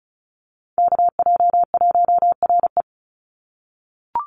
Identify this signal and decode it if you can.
Morse code, heard as a keyed tone, about 34 wpm, 700 Hz. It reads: KJ1RE
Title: Call Signs - 35wpm